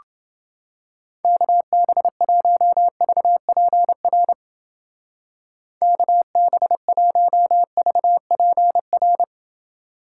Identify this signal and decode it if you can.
Morse code, heard as a keyed tone, about 29 wpm, 700 Hz. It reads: KB1VPR KB1VPR